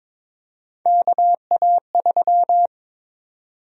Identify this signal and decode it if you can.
Morse code, heard as a keyed tone, about 22 wpm, 700 Hz. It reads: KA3
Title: USA Call Sign Prefixes - 22wpm